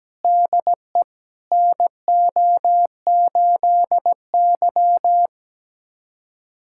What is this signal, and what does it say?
Morse code, heard as a keyed tone, about 17 wpm, 700 Hz. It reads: DE NO8Y
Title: USA Callsign DE Callsign - 4 Characters - 17wpm